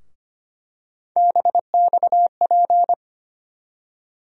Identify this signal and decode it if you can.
Morse code, heard as a keyed tone, about 25 wpm, 700 Hz. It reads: BXP
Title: USA Call Sign Postfixes - 3 Letters 25wpm